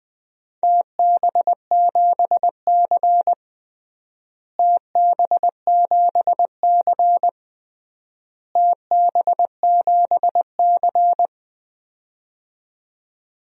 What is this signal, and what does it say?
Morse code, heard as a keyed tone, about 20 wpm, 700 Hz. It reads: TB7C TB7C TB7C
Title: DXCC-100 Call Signs - 4 Characters - Repeated 3x - 20wpm